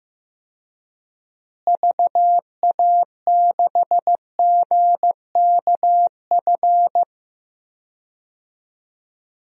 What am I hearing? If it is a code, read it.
Morse code, heard as a keyed tone, about 15 wpm, 700 Hz. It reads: VA6GKF
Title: DXCC-100 Call Signs - 6 Characters - Repeated 3x - 15wpm